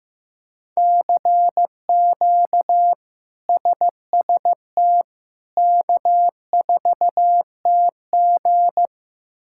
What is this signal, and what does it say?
Morse code, heard as a keyed tone, about 15 wpm, 700 Hz. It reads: CQ SST K4TG
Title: K1USN SST QSO Practice - 15wpm